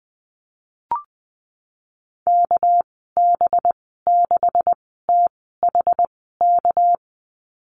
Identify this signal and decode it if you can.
Morse code, heard as a keyed tone, about 20 wpm, spaced as for 15 wpm, 700 Hz. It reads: KB6THK
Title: 238 Lesson 38g - Letter q - Call Signs 20wpm